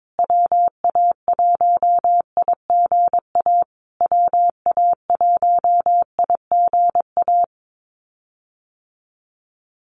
Morse code, heard as a keyed tone, about 22 wpm, 700 Hz. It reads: WA1IGA WA1IGA
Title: USA Call Signs Repeated 3x 22wpm